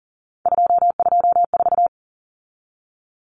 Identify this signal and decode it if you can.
Morse code, heard as a keyed tone, about 40 wpm, 700 Hz. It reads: 224
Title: Common Cut Numbers - 3-Digit 40wpm